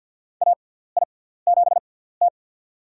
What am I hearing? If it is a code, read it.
Morse code, heard as a keyed tone, about 51 wpm, spaced as for 17 wpm, 700 Hz. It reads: AI7T
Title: USA Call Signs - 4 Characters - 6x Character Spacing - 50wpm